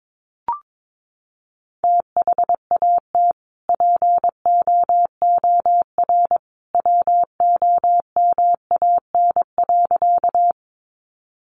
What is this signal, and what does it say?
Morse code, heard as a keyed tone, about 22 wpm, 700 Hz. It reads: THAT POOR WOMAN.
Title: Sentences from Top 1100 Words 22wpm